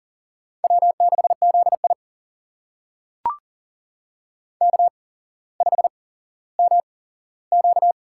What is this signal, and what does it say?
Morse code, heard as a keyed tone, about 40 wpm, spaced as for 17 wpm, 700 Hz. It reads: W6ZI K5MQ
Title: USA Call Signs - 4 Characters - 8x Character Spacing - 40wpm